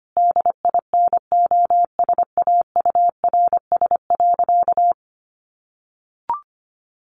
Morse code, heard as a keyed tone, about 25 wpm, 700 Hz. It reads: DINOSAURS.
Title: A Sound Of Thunder 25wpm